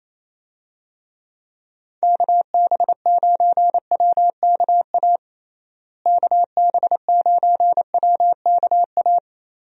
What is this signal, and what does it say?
Morse code, heard as a keyed tone, about 27 wpm, 700 Hz. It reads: KB9WKA KB9WKA
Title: USA Call Signs - 6 Characters - Repeated 3x - 28wpm